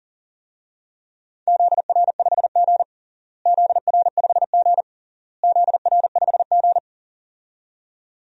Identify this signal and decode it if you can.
Morse code, heard as a keyed tone, about 40 wpm, 700 Hz. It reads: ZR5G ZR5G ZR5G
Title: DXCC-100 Call Signs - 4 Characters - Repeated 3x - 40wpm